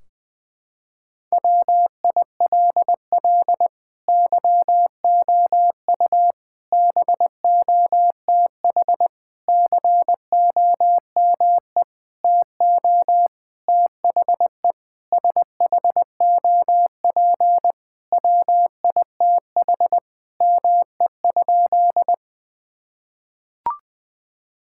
Morse code, heard as a keyed tone, about 20 wpm, 700 Hz. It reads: WILL YOU BOTH COME TO THE SHOP WITH ME?
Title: Sentences from Top 1600 Words 20wpm